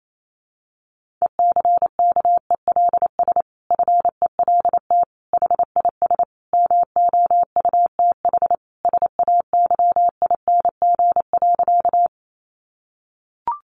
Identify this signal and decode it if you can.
Morse code, heard as a keyed tone, about 28 wpm, 700 Hz. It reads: ECKELS FELT HIS MOUTH SAYING.
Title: A Sound Of Thunder 28wpm